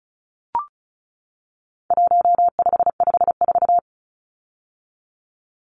Morse code, heard as a keyed tone, about 34 wpm, 700 Hz. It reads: 1554
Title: Numbers - 4-Digit 35wpm